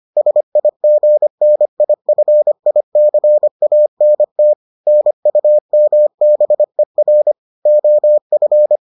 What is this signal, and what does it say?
Morse code, heard as a keyed tone, about 25 wpm, 600 Hz. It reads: SIGNIFICANT NUMBER OF